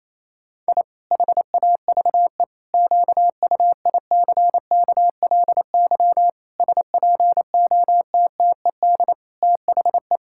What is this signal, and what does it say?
Morse code, heard as a keyed tone, about 28 wpm, 700 Hz. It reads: I HAVE QUICKLY SPOTTED THE
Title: Pangram Challenge v2 - 28wpm